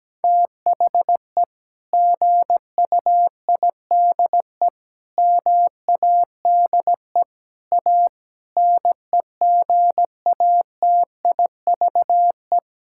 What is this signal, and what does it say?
Morse code, heard as a keyed tone, about 17 wpm, 700 Hz. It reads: THE GUIDE MADE A NEGATIVE
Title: Sentences from Top 1000 Words - Review Part 6 - 17wpm